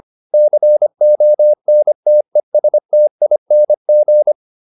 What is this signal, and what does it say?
Morse code, heard as a keyed tone, about 25 wpm, 600 Hz. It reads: CONTESTING